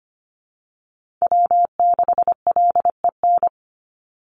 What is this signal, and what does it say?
Morse code, heard as a keyed tone, about 25 wpm, 700 Hz. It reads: W6LEN
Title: Call Signs - 25wpm